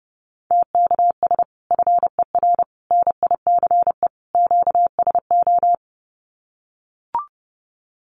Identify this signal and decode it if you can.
Morse code, heard as a keyed tone, about 30 wpm, 700 Hz. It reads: TKS FER NICE QSO
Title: Common QSO Phrases 30wpm